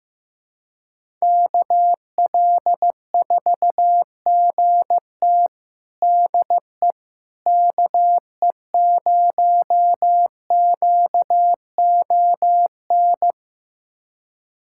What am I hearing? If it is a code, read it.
Morse code, heard as a keyed tone, about 16 wpm, 700 Hz. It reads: KL4GT DE KE0QON
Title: USA Callsign DE Callsign 15wpm